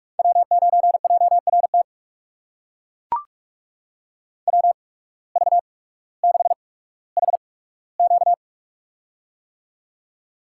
Morse code, heard as a keyed tone, about 42 wpm, spaced as for 19 wpm, 700 Hz. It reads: W9JRT WV6HQ
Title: USA Call Signs - 5 Characters - 8x Character Spacing - 45wpm